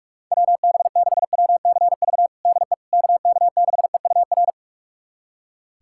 Morse code, heard as a keyed tone, about 48 wpm, 700 Hz. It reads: WB6WCV DE KK6EUR
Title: USA Callsign DE Callsign - 6 Characters - 45wpm